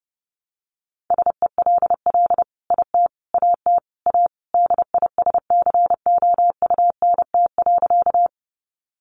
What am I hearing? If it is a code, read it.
Morse code, heard as a keyed tone, about 30 wpm, 700 Hz. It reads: SELL IT AT A DISCOUNT.
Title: Sentences from Top 1200 Words 30wpm